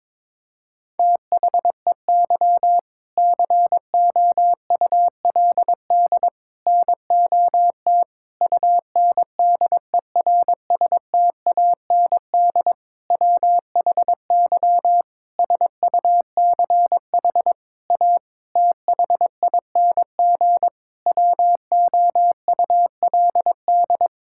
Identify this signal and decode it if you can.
Morse code, heard as a keyed tone, about 22 wpm, 700 Hz. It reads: THEY COULD NOT UNDERSTAND WHY SUCH A THING WOULD
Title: Sentences from Top 1000 Words - Review Part 6 - 22wpm